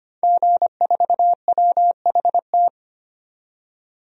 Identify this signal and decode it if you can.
Morse code, heard as a keyed tone, about 25 wpm, 700 Hz. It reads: G4WHT
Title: DXCC-100 Call Signs - 5 Characters - 25wpm